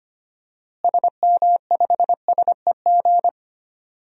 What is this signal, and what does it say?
Morse code, heard as a keyed tone, about 25 wpm, 700 Hz. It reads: SM5SEG
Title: DXCC-100 Call Signs - 6 Characters - Repeated 3x - 25wpm